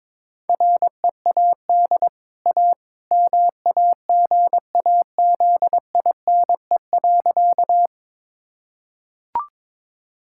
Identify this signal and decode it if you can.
Morse code, heard as a keyed tone, about 21 wpm, 690 Hz. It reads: READ A MAGAZINE.
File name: practice / 22wpm / Easy Sentences with Hints - 22wpm